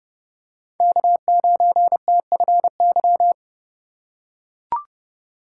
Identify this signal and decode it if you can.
Morse code, heard as a keyed tone, about 30 wpm, 700 Hz. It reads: K9TFY